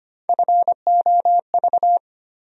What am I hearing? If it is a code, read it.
Morse code, heard as a keyed tone, about 25 wpm, 700 Hz. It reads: FOV
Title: USA Call Sign Postfixes - 3 Letters 25wpm